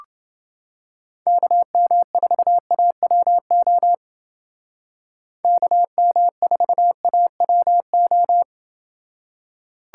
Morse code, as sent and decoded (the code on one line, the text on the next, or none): -.- -- ....- .- .-- --- / -.- -- ....- .- .-- ---
KM4AWO KM4AWO